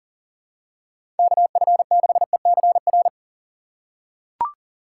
Morse code, heard as a keyed tone, about 40 wpm, 700 Hz. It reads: KF6ECR
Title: USA Call Signs - 6 Characters - 4x Character Spacing - 40wpm